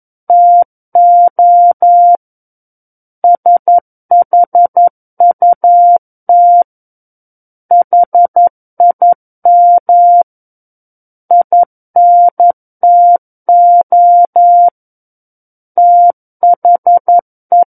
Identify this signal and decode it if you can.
Morse code, heard as a keyed tone, about 11 wpm, 700 Hz. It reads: TO SHUT HIM INTO THE
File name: Book One / 02-Chapter 2 - 11 WPM